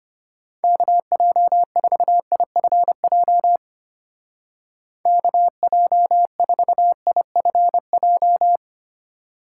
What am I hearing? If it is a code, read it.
Morse code, heard as a keyed tone, about 32 wpm, 700 Hz. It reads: KJ4IFJ KJ4IFJ